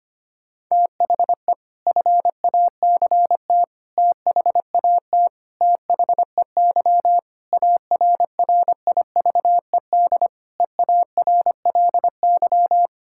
Morse code, heard as a keyed tone, about 25 wpm, 700 Hz. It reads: THE FACT THAT THEY ARRIVED EARLY
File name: practice / 25wpm / Sentences with 4-Word Phrases - 25wpm